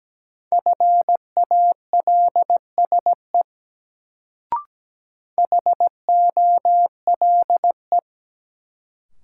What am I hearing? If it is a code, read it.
Morse code, heard as a keyed tone, about 17 wpm, 700 Hz. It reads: FALSE HOLE
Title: Sentences from Top 1600 Words - New Word Drill - 17wpm